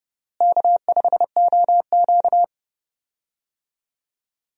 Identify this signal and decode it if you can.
Morse code, heard as a keyed tone, about 31 wpm, 700 Hz. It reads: K5OQ
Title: USA Call Signs - 4 Characters - Repeated 2x - 30wpm